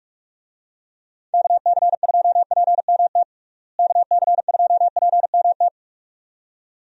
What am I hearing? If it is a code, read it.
Morse code, heard as a keyed tone, about 41 wpm, 700 Hz. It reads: KC2PMT KC2PMT
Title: USA Call Signs - 6 Characters - Repeated 2x - 45wpm